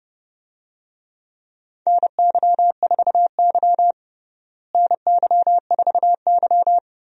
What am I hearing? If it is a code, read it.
Morse code, heard as a keyed tone, about 30 wpm, 700 Hz. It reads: NY4Y NY4Y
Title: USA Call Signs - 4 Characters - Repeated 3x - 30wpm